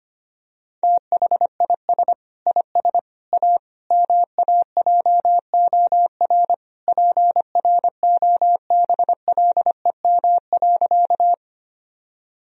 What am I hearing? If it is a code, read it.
Morse code, heard as a keyed tone, about 25 wpm, 700 Hz. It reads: THIS IS A MAJOR PROBLEM.
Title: Sentences from Top 1000 Words - Review Part 4 - 25wpm